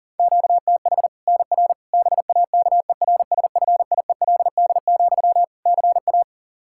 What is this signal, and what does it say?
Morse code, heard as a keyed tone, about 40 wpm, 700 Hz. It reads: QTH NR BAKERSFIELD, CA
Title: K1USN SST QSO Practice 40wpm